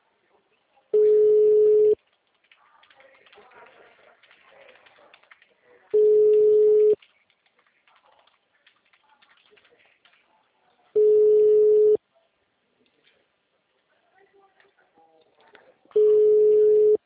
Ringback tone (Europe, 425 Hz).